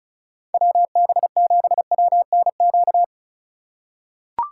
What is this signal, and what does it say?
Morse code, heard as a keyed tone, about 34 wpm, 700 Hz. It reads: WB7WNQ
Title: USA Call Signs - 6 Characters - 2x Character Spacing - 35wpm